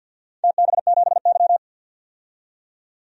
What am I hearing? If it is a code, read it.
Morse code, heard as a keyed tone, about 51 wpm, 700 Hz. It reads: TB7Y
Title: DXCC-100 Call Signs - 4 Characters - 50wpm